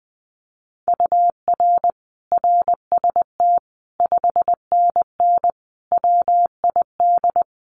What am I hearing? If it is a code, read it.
Morse code, heard as a keyed tone, about 20 wpm, 700 Hz. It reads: UR RST 5NN WID
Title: Common QSO Phrases 20wpm